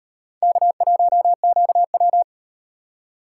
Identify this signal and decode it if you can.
Morse code, heard as a keyed tone, about 36 wpm, 700 Hz. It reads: K1QW